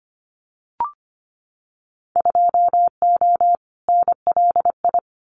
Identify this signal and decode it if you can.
Morse code, heard as a keyed tone, about 25 wpm, 700 Hz. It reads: 2O NLI
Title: Winter Field Day Exchanges 25wpm